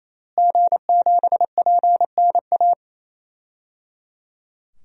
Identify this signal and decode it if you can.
Morse code, heard as a keyed tone, about 28 wpm, 700 Hz. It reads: G7PNA
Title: DXCC-100 Call Signs - 5 Characters - Repeated 3x - 28wpm